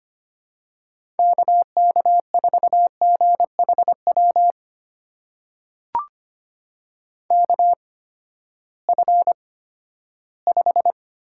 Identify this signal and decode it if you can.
Morse code, heard as a keyed tone, about 25 wpm, spaced as for 14 wpm, 700 Hz. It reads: KK4GHW KF5
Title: USA Call Signs - 6 Characters - 8x Character Spacing - 25wpm